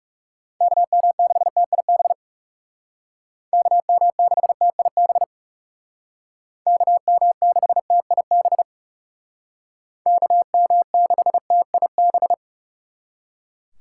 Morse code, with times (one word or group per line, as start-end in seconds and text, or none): 0.6-2.1 KM6TIB
3.5-5.2 KM6TIB
6.7-8.6 KM6TIB
10.1-12.3 KM6TIB